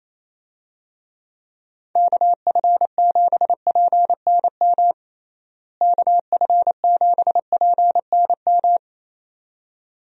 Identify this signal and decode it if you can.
Morse code, heard as a keyed tone, about 28 wpm, 700 Hz. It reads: KF7PNM KF7PNM
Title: USA Call Signs - 6 Characters - Repeated 2x - 28wpm